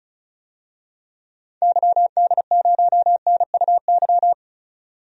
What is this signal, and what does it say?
Morse code, heard as a keyed tone, about 36 wpm, 700 Hz. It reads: YD0NUY
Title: DXCC-100 Call Signs - 6 Characters - 35wpm